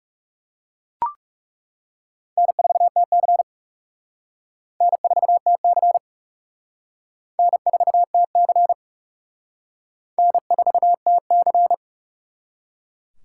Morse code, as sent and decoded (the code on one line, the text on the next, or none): -. ....- - -.-. / -. ....- - -.-. / -. ....- - -.-. / -. ....- - -.-.
N4TC N4TC N4TC N4TC